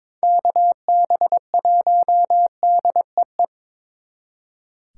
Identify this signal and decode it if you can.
Morse code, heard as a keyed tone, about 26 wpm, 700 Hz. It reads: KB1DEE
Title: Speed Racing - USA 2x3 Call Signs 22wpm